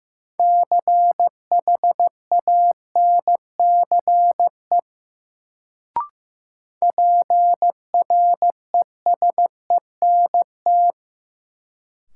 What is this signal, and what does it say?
Morse code, heard as a keyed tone, about 15 wpm, 700 Hz. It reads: CHANCE PRESENT
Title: Sentences from Top 800 Words - New Word Drill - 15wpm